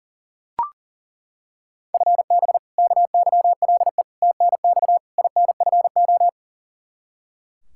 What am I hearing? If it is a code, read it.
Morse code, heard as a keyed tone, about 40 wpm, 700 Hz. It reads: FB KYLE TNX INFO
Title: Practice QSO - Exchange 3 - More Info - 40wpm